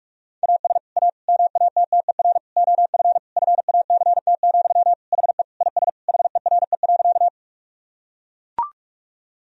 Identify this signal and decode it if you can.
Morse code, heard as a keyed tone, about 41 wpm, 700 Hz. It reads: AS A MATTER OF FACT, HE IS HERE.
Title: Sentences from Top 1000 Words - Review Part 3 - 45wpm